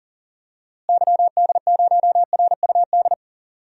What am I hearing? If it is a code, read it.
Morse code, heard as a keyed tone, about 40 wpm, 700 Hz. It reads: YD0RUD
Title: DXCC-100 Call Signs - 6 Characters - Repeated 2x - 40wpm